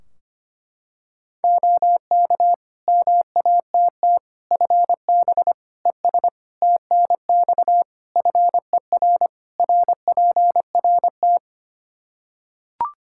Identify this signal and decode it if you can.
Morse code, heard as a keyed tone, about 25 wpm, 700 Hz. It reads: OK MATT FB ES TNX FER RPRT
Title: Practice QSO - Exchange 2 - Info - 25wpm